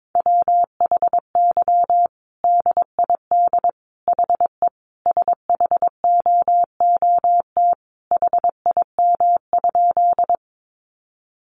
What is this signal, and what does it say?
Morse code, heard as a keyed tone, about 22 wpm, 700 Hz. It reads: WHY DID HE SHOOT HIM?
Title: Sentences from Top 1200 Words 22wpm